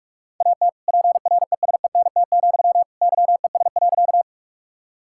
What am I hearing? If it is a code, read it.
Morse code, heard as a keyed tone, about 46 wpm, 700 Hz. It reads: AT PRESENT, YES.
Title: Sentences from Top 1000 Words - Review Part 5 - 45wpm